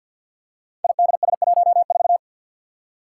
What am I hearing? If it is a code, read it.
Morse code, heard as a keyed tone, about 47 wpm, 700 Hz. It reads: IDS14
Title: Five Challenging Letter-Number Combinations 50wpm